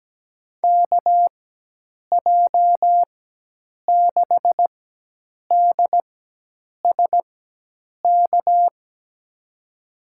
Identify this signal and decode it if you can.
Morse code, heard as a keyed tone, about 17 wpm, spaced as for 8 wpm, 700 Hz. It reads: KJ6DSK
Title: USA Call Signs - 6 Characters - 4x Character Spacing - 17wpm